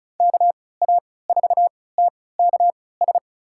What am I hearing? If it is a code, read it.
Morse code, heard as a keyed tone, about 36 wpm, spaced as for 20 wpm, 700 Hz. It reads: KA4TKS
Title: USA Call Signs - 3x Character Spacing - 35wpm